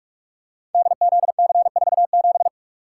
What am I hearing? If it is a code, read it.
Morse code, heard as a keyed tone, about 44 wpm, 700 Hz. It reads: DZC47
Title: Code Groups - 5 Letter:Numbers - 45wpm